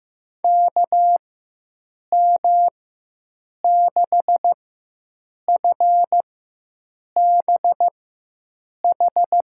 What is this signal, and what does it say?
Morse code, heard as a keyed tone, about 15 wpm, spaced as for 7 wpm, 700 Hz. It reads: KM6FBH
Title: USA Call Signs - 4x Character Spacing - 15wpm